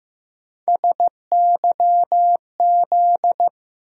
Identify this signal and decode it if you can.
Morse code, heard as a keyed tone, about 15 wpm, 700 Hz. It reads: SYZ
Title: USA Call Sign Postfixes - 3 Letters 15wpm